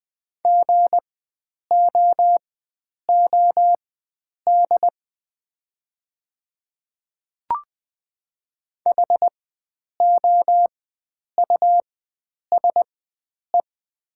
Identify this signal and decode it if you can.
Morse code, heard as a keyed tone, about 20 wpm, spaced as for 9 wpm, 700 Hz. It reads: GOOD HOUSE
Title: 4x Character Spacing - Top 300 Words - 20wpm